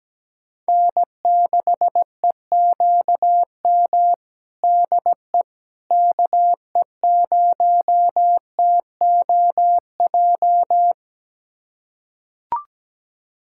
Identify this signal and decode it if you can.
Morse code, heard as a keyed tone, about 17 wpm, 700 Hz. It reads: N6EQM DE KE0TOJ
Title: USA Callsign DE Callsign 17wpm